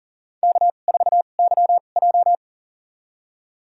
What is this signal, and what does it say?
Morse code, heard as a keyed tone, about 40 wpm, spaced as for 29 wpm, 700 Hz. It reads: K4YJ
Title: USA Call Signs - 4 Characters - 2x Character Spacing - 40wpm